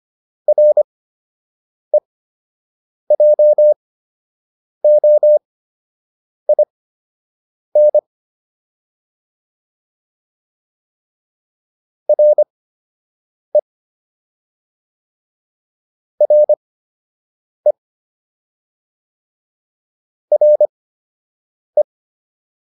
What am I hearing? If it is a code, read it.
Morse code, heard as a keyed tone, about 25 wpm, spaced as for 7 wpm, 600 Hz. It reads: REJOIN RE RE RE